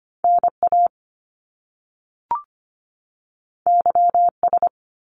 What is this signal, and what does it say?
Morse code, heard as a keyed tone, about 25 wpm, 700 Hz. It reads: NA YS
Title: Top Two Letter Combinations in Words 25wpm